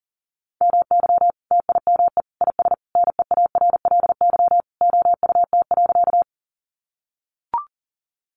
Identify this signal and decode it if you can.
Morse code, heard as a keyed tone, about 40 wpm, 700 Hz. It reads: MY TIME IS NEARLY OUT.
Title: Sentences from Top 1300 Words 40wpm